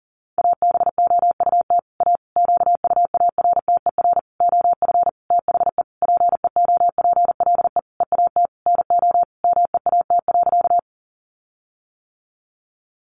Morse code, heard as a keyed tone, about 40 wpm, 700 Hz. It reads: ABOUT A QUARTER OF THE PEOPLE EAT NO MEAT.
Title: Sentences from Top 1300 Words 40wpm